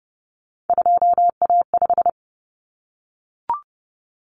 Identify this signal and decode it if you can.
Morse code, heard as a keyed tone, about 30 wpm, 700 Hz. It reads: 2A5
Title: Common Cut Numbers - 3-Digit 30wpm